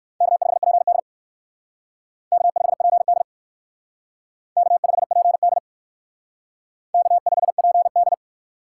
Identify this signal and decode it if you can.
Morse code, heard as a keyed tone, about 56 wpm, 700 Hz. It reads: K5PD K5PD K5PD K5PD